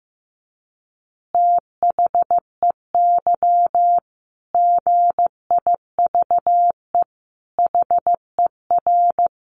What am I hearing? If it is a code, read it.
Morse code, heard as a keyed tone, about 15 wpm, 700 Hz. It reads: THEY GIVE HER
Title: Easy Phrases 15wpm